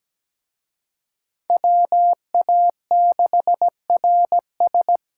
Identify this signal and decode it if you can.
Morse code, heard as a keyed tone, about 20 wpm, 700 Hz. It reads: WA6RS
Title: Speed Racing - USA 2x2 Call Signs 17wpm